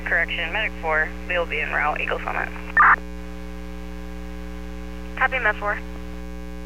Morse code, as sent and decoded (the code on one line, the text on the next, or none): none